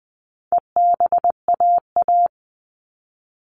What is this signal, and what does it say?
Morse code, heard as a keyed tone, about 20 wpm, 700 Hz. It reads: EBAA
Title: All Cut Numbers - 4-Digit 20wpm